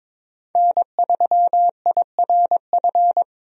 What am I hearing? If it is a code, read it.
Morse code, heard as a keyed tone, about 22 wpm, 700 Hz. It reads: N3IRF
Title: USA Call Signs - 5 Characters - 22wpm